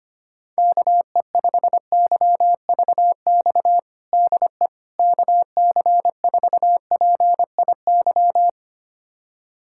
Morse code, heard as a keyed tone, about 25 wpm, 700 Hz. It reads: KE5YVX DE KC4PIY
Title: USA Callsign DE Callsign - 6 Characters - 25wpm